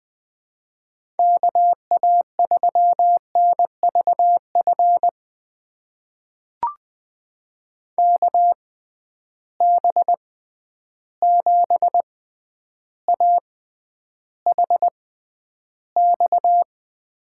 Morse code, heard as a keyed tone, about 20 wpm, spaced as for 10 wpm, 700 Hz. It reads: KA3NVF KB7AHX
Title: USA Call Signs - 6 Characters - 6x Character Spacing - 20wpm